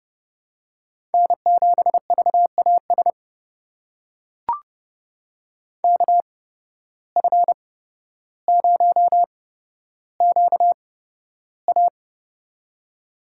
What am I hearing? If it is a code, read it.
Morse code, heard as a keyed tone, about 31 wpm, spaced as for 13 wpm, 700 Hz. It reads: N7VAS KF0QA
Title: USA Call Signs - 5 Characters - 8x Character Spacing - 30wpm